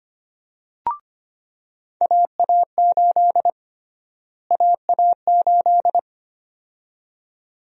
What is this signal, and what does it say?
Morse code, heard as a keyed tone, about 25 wpm, 700 Hz. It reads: AA8 AA8
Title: USA Call Sign Prefixes - Repeated 2x - 25wpm